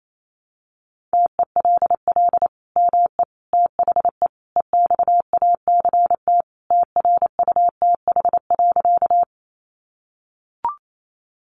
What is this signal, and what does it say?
Morse code, heard as a keyed tone, about 28 wpm, 700 Hz. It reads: TELL ME THE EXACT TRUTH.
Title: Sentences from Top 1300 Words 28wpm